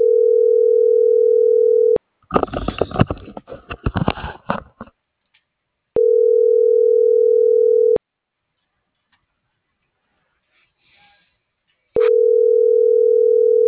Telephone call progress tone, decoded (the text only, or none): ringback tone